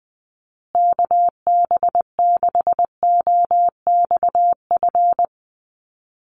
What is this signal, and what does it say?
Morse code, heard as a keyed tone, about 20 wpm, 700 Hz. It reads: KB6OXF
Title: USA Call Signs Repeated 2x 20wpm